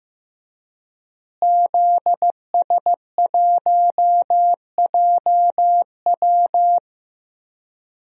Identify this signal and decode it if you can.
Morse code, heard as a keyed tone, about 15 wpm, 700 Hz. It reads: ZS1JW
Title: DXCC-100 Call Signs - 5 Characters - Repeated 3x - 15wpm